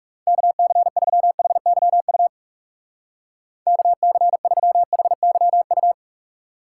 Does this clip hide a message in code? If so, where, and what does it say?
Yes: Morse code from 0.3 to 5.9 s, heard as a keyed tone, about 38 wpm, 700 Hz. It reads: KC3HYU KC3HYU